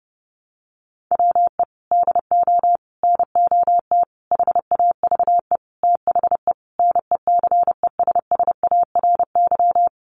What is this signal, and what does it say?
Morse code, heard as a keyed tone, about 30 wpm, 700 Hz. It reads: WE DO NOT HAVE THE NECESSARY
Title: Sentences from Top 1200 Words 30wpm